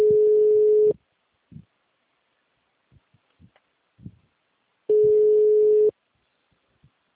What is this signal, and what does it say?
Ringback tone (Europe, 425 Hz).